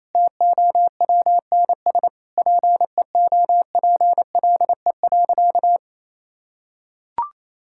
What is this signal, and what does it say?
Morse code, heard as a keyed tone, about 28 wpm, 700 Hz. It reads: TOWNS PEOPLE.
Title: Sentences from Top 1000 Words - Review Part 6 - 28wpm